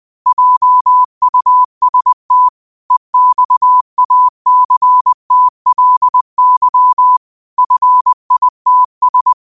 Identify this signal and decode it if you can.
Morse code, heard as a keyed tone, about 20 wpm, 1000 Hz. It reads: JUST EXACTLY FITS
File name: qotdfiles / 20wpm